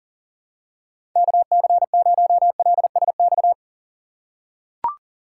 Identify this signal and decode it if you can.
Morse code, heard as a keyed tone, about 40 wpm, 700 Hz. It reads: KC0LSX
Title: USA Call Signs - 2x Character Spacing - 40wpm